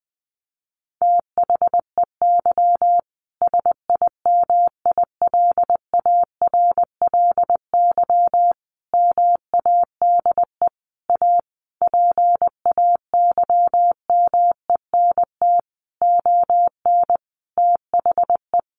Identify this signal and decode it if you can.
Morse code, heard as a keyed tone, about 20 wpm, 700 Hz. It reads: THEY SIMILARLY MADE A PAYMENT ON THE
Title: Sentences from Top 1400 Words 20wpm